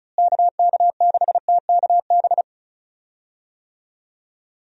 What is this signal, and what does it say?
Morse code, heard as a keyed tone, about 36 wpm, 700 Hz. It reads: KK6TKB
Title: USA Call Signs - 6 Characters - Repeated 3x - 35wpm